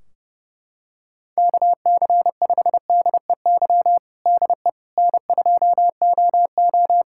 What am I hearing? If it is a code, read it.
Morse code, heard as a keyed tone, about 30 wpm, 700 Hz. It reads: KC5DEY DE N2OO
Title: K1USN SST QSO Practice 30wpm